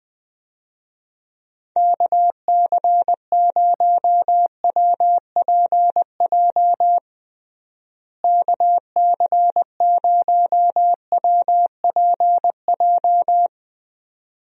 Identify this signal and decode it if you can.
Morse code, heard as a keyed tone, about 20 wpm, 700 Hz. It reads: KC0WPJ KC0WPJ
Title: USA Call Signs - 6 Characters - Repeated 2x - 20wpm